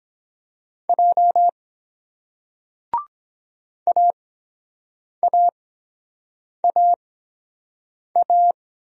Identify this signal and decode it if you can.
Morse code, heard as a keyed tone, about 22 wpm, spaced as for 7 wpm, 700 Hz. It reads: J AAAA